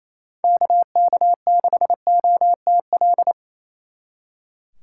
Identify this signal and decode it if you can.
Morse code, heard as a keyed tone, about 28 wpm, 700 Hz. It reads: KK6OTL
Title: Speed Racing - USA 2x3 Call Signs 28wpm